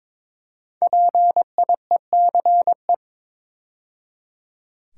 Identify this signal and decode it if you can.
Morse code, heard as a keyed tone, about 22 wpm, 700 Hz. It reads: PIECE
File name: 5 Letter Words - First 3 Letters Given - 22wpm